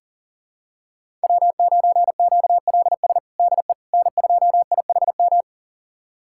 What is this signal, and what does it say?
Morse code, heard as a keyed tone, about 40 wpm, 700 Hz. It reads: W9QLS DE N2IHM
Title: USA Callsign DE Callsign - 5 Characters - 40wpm